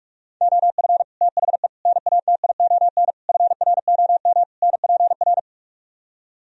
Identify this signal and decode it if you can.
Morse code, heard as a keyed tone, about 48 wpm, 700 Hz. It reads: OF THE NATION FROM NPR